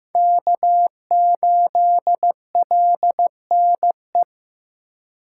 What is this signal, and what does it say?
Morse code, heard as a keyed tone, about 16 wpm, 700 Hz. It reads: K8LNE
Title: USA Call Signs - 5 Characters - 2x Character Spacing - 15wpm